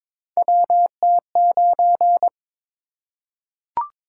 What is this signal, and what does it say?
Morse code, heard as a keyed tone, about 22 wpm, 700 Hz. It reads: WT9
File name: practice / 22wpm / Three Letter-Number Combinations - 22wpm